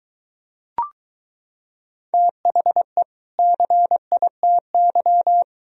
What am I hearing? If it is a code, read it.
Morse code, heard as a keyed tone, about 23 wpm, 700 Hz. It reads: THE CITY